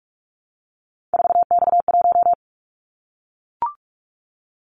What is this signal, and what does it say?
Morse code, heard as a keyed tone, about 44 wpm, 700 Hz. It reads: 4X1
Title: Callsign Prefixes for Top 50 DXCC - 45wpm